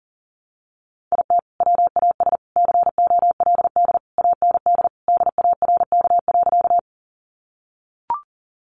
Morse code, heard as a keyed tone, about 37 wpm, 700 Hz. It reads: IT WAS COLD AND DARK.
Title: Sentences from Top 500 Words 40wpm